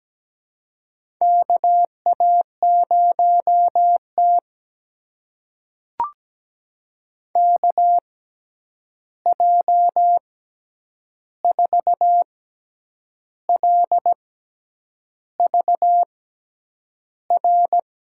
Morse code, heard as a keyed tone, about 17 wpm, spaced as for 8 wpm, 700 Hz. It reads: KA0T KJ4LVR
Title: USA Call Signs - 6x Character Spacing - 17wpm